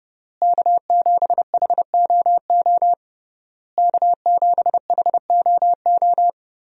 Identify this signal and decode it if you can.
Morse code, heard as a keyed tone, about 30 wpm, 700 Hz. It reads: K7HOO K7HOO
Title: USA Call Signs - 5 Characters - Repeated 3x - 30wpm